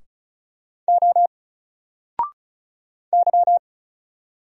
Morse code, heard as a keyed tone, about 33 wpm, 700 Hz. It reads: O Y
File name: Single-Letter-Number-35wpm